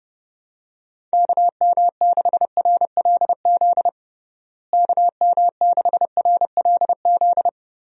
Morse code, heard as a keyed tone, about 30 wpm, 700 Hz. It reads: KM6RLZ KM6RLZ
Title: USA Call Signs - 6 Characters - Repeated 3x - 30wpm